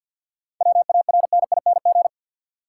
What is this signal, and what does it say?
Morse code, heard as a keyed tone, about 50 wpm, 700 Hz. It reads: WARNING
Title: Seven Letter Words - 50wpm